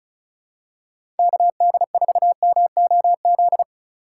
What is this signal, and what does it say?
Morse code, heard as a keyed tone, about 36 wpm, 700 Hz. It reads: KD4MOZ
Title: USA Call Signs - 6 Characters - 6x Character Spacing - 35wpm